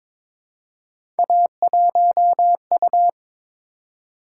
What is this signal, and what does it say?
Morse code, heard as a keyed tone, about 22 wpm, 700 Hz. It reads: A1U
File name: practice / 22wpm / Three Letter-Number Combinations - 22wpm